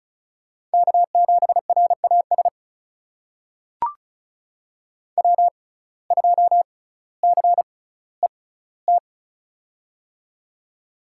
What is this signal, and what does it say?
Morse code, heard as a keyed tone, about 34 wpm, spaced as for 18 wpm, 700 Hz. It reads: K7RAS W2CET
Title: USA Call Signs - 6x Character Spacing - 35wpm